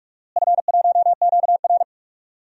Morse code, heard as a keyed tone, about 43 wpm, 700 Hz. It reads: F1QR
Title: DXCC-100 Call Signs - 4 Characters - Repeated 3x - 45wpm